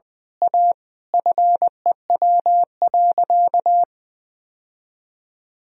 Morse code, heard as a keyed tone, about 21 wpm, 700 Hz. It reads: A FEW.
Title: Sentences from Top 1000 Words - Review Part 1 - 20wpm